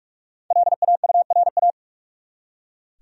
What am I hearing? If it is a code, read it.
Morse code, heard as a keyed tone, about 46 wpm, 700 Hz. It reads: LAURA
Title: USA Common Names - 45wpm